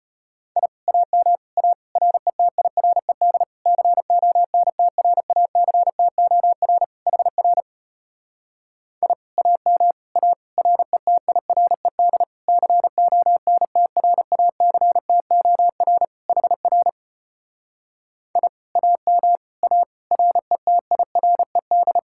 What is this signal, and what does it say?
Morse code, heard as a keyed tone, about 33 wpm, 700 Hz. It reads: I AM A RETIRED CONTRACTOR HR I AM A RETIRED CONTRACTOR HR I AM A RETIRED